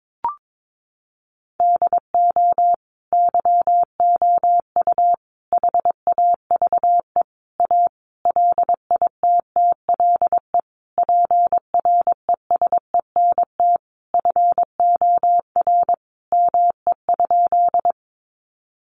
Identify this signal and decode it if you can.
Morse code, heard as a keyed tone, about 22 wpm, 700 Hz. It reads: DO YOU HAVE A LITTLE PRESENT FOR ME?
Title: Sentences from Top 800 Words 22wpm